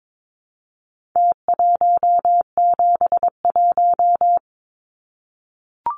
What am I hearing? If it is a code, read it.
Morse code, heard as a keyed tone, about 22 wpm, 700 Hz. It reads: T171
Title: Numbers - Contest Style - 4-Digit 22wpm